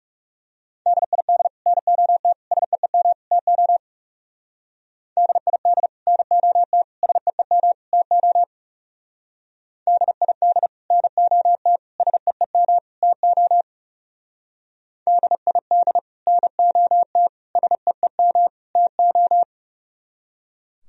Morse code, heard as a keyed tone, about 34 wpm, 700 Hz. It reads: DID NOT SEEM TO DID NOT SEEM TO DID NOT SEEM TO DID NOT SEEM TO
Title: Speed-Racing - Sets of 4 Words 30wpm